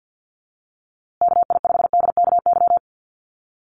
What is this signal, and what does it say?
Morse code, heard as a keyed tone, about 50 wpm, 700 Hz. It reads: KI5DKY
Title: USA Call Signs Repeated 2x 50wpm